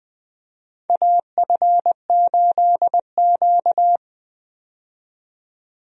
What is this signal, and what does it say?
Morse code, heard as a keyed tone, about 20 wpm, 700 Hz. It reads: AF8Q
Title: USA Call Signs - 4 Characters - 20wpm